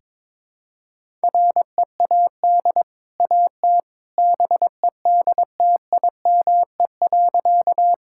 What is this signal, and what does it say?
Morse code, heard as a keyed tone, about 22 wpm, 690 Hz. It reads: READ AT BEDTIME.
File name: practice / 22wpm / Easy Sentences with Hints - 22wpm